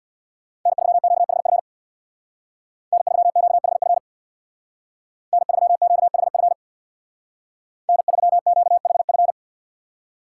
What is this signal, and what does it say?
Morse code, heard as a keyed tone, about 64 wpm, 700 Hz. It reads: N3QHF N3QHF N3QHF N3QHF